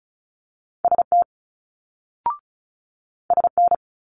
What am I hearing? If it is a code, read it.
Morse code, heard as a keyed tone, about 36 wpm, 700 Hz. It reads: ST SN
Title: Multi-letter Phonemes Quick Fire 35wpm